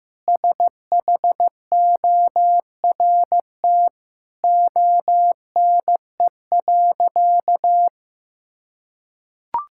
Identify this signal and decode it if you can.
Morse code, heard as a keyed tone, about 15 wpm, 700 Hz. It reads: SHORT ONE.
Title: Sentences from Top 1600 Words 15wpm